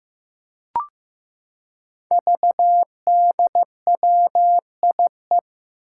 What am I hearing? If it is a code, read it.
Morse code, heard as a keyed tone, about 15 wpm, 700 Hz. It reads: VDWIE
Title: Five Challenging Letter-Number Combinations 15wpm